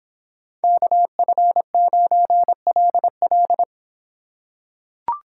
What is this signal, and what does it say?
Morse code, heard as a keyed tone, about 26 wpm, 700 Hz. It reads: KF9LL